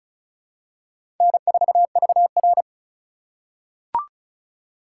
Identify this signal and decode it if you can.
Morse code, heard as a keyed tone, about 36 wpm, 700 Hz. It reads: N4VR
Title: USA Call Signs - 8x Character Spacing - 35wpm